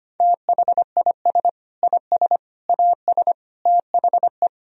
Morse code, heard as a keyed tone, about 25 wpm, 700 Hz. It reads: THIS IS AS THE
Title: Sentences from Top 1000 Words - Review Part 3 - 25wpm